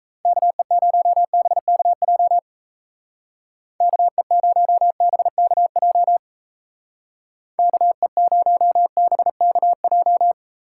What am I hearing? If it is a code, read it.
Morse code, heard as a keyed tone, about 33 wpm, 700 Hz. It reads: KE0BKJ KE0BKJ KE0BKJ
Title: Speed Racing - USA 2x3 Call Signs 28wpm